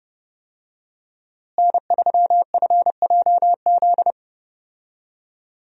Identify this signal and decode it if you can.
Morse code, heard as a keyed tone, about 30 wpm, 700 Hz. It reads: N3FJZ
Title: USA Call Signs - 5 Characters - Repeated 2x - 30wpm